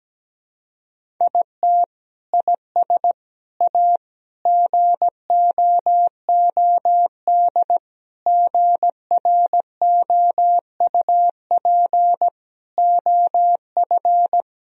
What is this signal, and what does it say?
Morse code, heard as a keyed tone, about 17 wpm, 700 Hz. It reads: IT IS A GOOD GROUP OF
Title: Sentences from Top 1000 Words - Review Part 1 - 17wpm